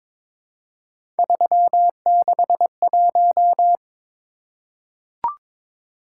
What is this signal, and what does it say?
Morse code, heard as a keyed tone, about 22 wpm, 700 Hz. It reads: 361
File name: practice / 22wpm / Three Letter-Number Combinations - 22wpm